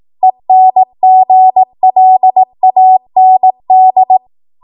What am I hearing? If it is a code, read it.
Morse code, heard as a keyed tone, about 18 wpm, 750 Hz. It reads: ENGLAND